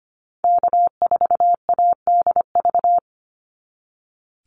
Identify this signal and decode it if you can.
Morse code, heard as a keyed tone, about 25 wpm, 700 Hz. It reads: K4ADV
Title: NA SOTA Activator Callsigns 25wpm